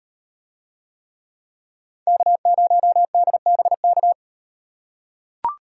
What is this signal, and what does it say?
Morse code, heard as a keyed tone, about 36 wpm, 700 Hz. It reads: K0DBK